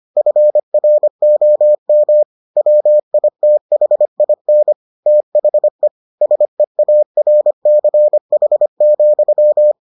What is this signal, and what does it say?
Morse code, heard as a keyed tone, about 25 wpm, 600 Hz. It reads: FROM WITHIN THE SEARCH,